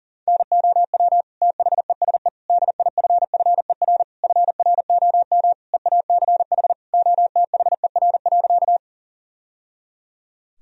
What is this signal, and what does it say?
Morse code, heard as a keyed tone, about 40 wpm, 700 Hz. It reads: NOW THESE DIFFER FROM EACH OTHER.
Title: Sentences from Top 1000 Words - Review Part 1 - 40wpm